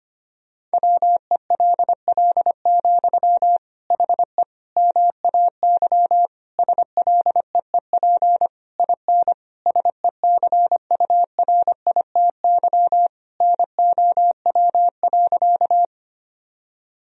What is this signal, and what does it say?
Morse code, heard as a keyed tone, about 25 wpm, 700 Hz. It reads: WELL, HE MAY SLEEP IN SECURITY NOW.
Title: Sentences from Top 1000 Words - Review Part 4 - 25wpm